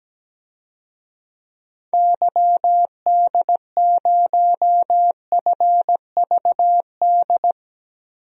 Morse code, heard as a keyed tone, about 17 wpm, 700 Hz. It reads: YD0FVD
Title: DXCC-100 Call Signs - 6 Characters - 17wpm